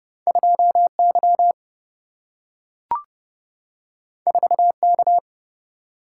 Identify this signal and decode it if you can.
Morse code, heard as a keyed tone, about 30 wpm, 700 Hz. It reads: 2 Y 4 K
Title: Single-Letter Single-Number 30wpm